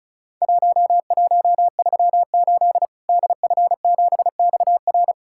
Morse code, heard as a keyed tone, about 36 wpm, 700 Hz. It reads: 1138 DF7XR
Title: Worked All Europe DX Contest - QTC Exchange - 35wpm